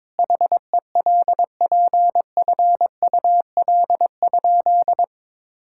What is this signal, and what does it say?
Morse code, heard as a keyed tone, about 22 wpm, 700 Hz. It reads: HELPFUL?
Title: Sentences from Top 1000 Words - Review Part 6 - 22wpm